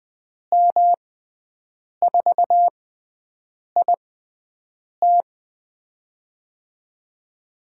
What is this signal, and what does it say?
Morse code, heard as a keyed tone, about 20 wpm, spaced as for 7 wpm, 700 Hz. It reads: M4IT
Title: USA Call Signs - 6x Character Spacing - 20wpm